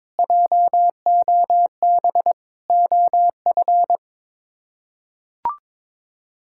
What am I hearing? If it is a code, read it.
Morse code, heard as a keyed tone, about 22 wpm, 700 Hz. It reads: JOB OF
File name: practice / 22wpm / Sets of 6 Words - 22wpm